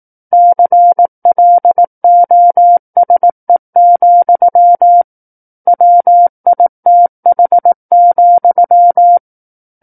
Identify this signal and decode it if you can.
Morse code, heard as a keyed tone, about 18 wpm, 700 Hz. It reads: CLOSE, WITH,
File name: 18 WPM